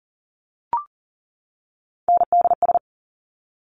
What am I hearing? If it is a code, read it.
Morse code, heard as a keyed tone, about 39 wpm, 690 Hz. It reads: NDS
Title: Top Three Letter Combinations in Words 40wpm